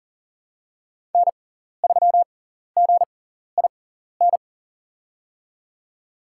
Morse code, heard as a keyed tone, about 40 wpm, spaced as for 14 wpm, 700 Hz. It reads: N3GIN
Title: USA Call Signs - 6x Character Spacing - 40wpm